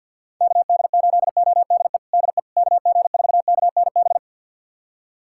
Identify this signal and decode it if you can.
Morse code, heard as a keyed tone, about 53 wpm, 700 Hz. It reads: KD8ODE DE KG4KNB
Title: USA Callsign DE Callsign - 6 Characters - 50wpm